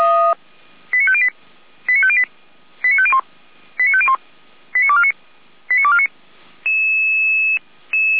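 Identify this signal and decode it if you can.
Five-tone selective call (ZVEI), heard as quick runs of five tones, tones of about 70 ms. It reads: 89508 89508 89501 89501 89249 89249
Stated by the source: SDR-received ZVEI five-tone call, code 89229.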